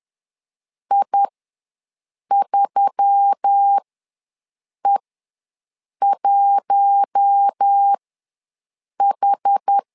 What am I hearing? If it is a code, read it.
Morse code, heard as a keyed tone, about 10 wpm, spaced as for 6 wpm, 800 Hz. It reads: I3E1H
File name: LEZIONI / 4 Q1S3 ws 10wpm 2437